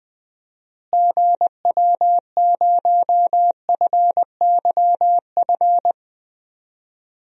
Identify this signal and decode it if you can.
Morse code, heard as a keyed tone, about 20 wpm, 700 Hz. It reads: GW0FYF
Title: DXCC-100 Call Signs - 6 Characters - Repeated 3x - 20wpm